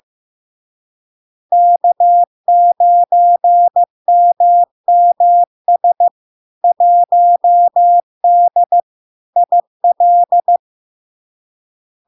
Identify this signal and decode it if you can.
Morse code, heard as a keyed tone, about 15 wpm, 700 Hz. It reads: K9MMS 1D IL
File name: LARK-FD-2022-Log-in-CW-15wpm-700-Hz